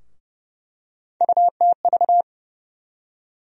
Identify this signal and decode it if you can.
Morse code, heard as a keyed tone, about 32 wpm, 700 Hz. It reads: UTV
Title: USA Call Sign Postfixes - 3 Letters 30wpm